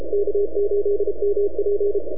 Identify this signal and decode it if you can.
Morse code, heard as a keyed tone, about 32 wpm, 410 Hz. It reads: K8MP
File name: NAQPCW-8-2009 / W1UJ NAQP CW 09 1930Z